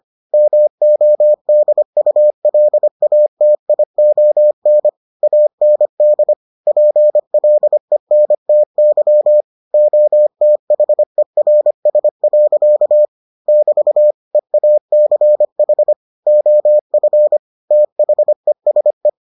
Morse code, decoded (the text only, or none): MODULATION AND PLENTY OTHERS. <BT> EACH OF THESE